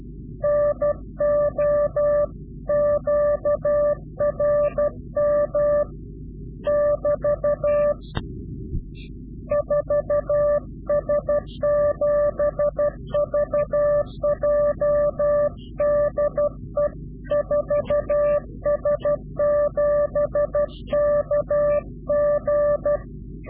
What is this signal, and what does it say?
Morse code, heard as a keyed tone, about 12 wpm, 590 Hz. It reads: NO QRM <BT> 4S7VJ DE 4S7KG
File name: CW-QSO / Directional Call / Directional QSO